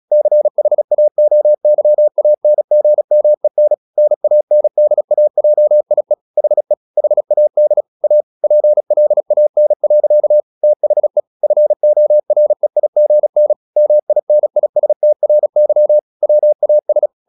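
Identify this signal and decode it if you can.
Morse code, heard as a keyed tone, about 36 wpm, 600 Hz. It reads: CHAOYANGMEN NANDAJIE HE HAD A PLAN. THE FOREIGN MINISTRY WAS
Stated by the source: Morse practice at 36 wpm, text of Ken Follett's novel Never, from CW med Gnister.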